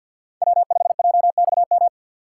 Morse code, heard as a keyed tone, about 62 wpm, 700 Hz. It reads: W5JXM
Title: Speed Racing - USA 1x3 Call Signs 50wpm